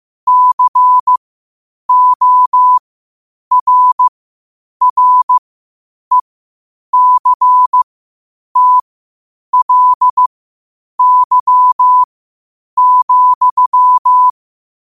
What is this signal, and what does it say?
Morse code, heard as a keyed tone, about 15 wpm, spaced as for 8 wpm, 1000 Hz. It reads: CORRECTLY,
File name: qotdfiles / 8wpm